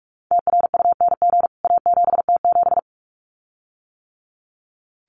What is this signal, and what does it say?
Morse code, heard as a keyed tone, about 44 wpm, 700 Hz. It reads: TRUNG A7T7
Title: CWT Exchange 45wpm